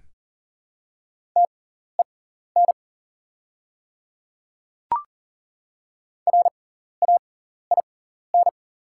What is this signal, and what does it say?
Morse code, heard as a keyed tone, about 40 wpm, spaced as for 14 wpm, 690 Hz. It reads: TEN RAIN
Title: 6x Character Spacing - Top 500 Words - 40wpm